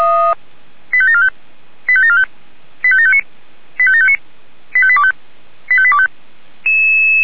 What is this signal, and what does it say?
Five-tone selective call (ZVEI), heard as quick runs of five tones, tones of about 70 ms. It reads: 85745 85745 85759 85759 85825 85825